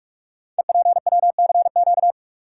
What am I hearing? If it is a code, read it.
Morse code, heard as a keyed tone, about 45 wpm, 700 Hz. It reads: EPWCQ
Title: Code Groups - 5 Letter:Numbers - 45wpm